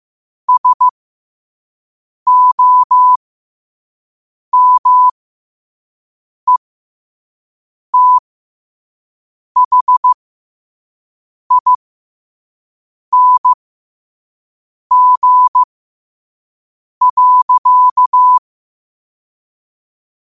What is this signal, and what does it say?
Morse code, heard as a keyed tone, about 15 wpm, spaced as for 5 wpm, 1000 Hz. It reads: SOMETHING.
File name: qotdfiles / 5wpm